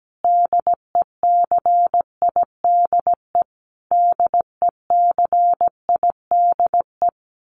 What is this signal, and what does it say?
Morse code, heard as a keyed tone, about 17 wpm, 700 Hz. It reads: DECIDE DECIDE
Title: Top 500 Words Repeated 17wpm